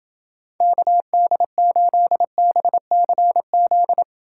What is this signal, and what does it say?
Morse code, heard as a keyed tone, about 29 wpm, 700 Hz. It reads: KD8BCZ